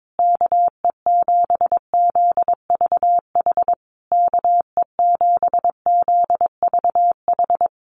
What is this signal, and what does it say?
Morse code, heard as a keyed tone, about 22 wpm, 700 Hz. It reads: KE7ZVH KE7ZVH
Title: USA Call Signs Repeated 3x 22wpm